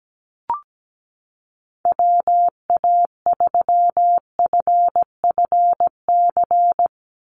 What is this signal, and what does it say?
Morse code, heard as a keyed tone, about 17 wpm, 700 Hz. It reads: WA3FFC
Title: USA Call Signs 17wpm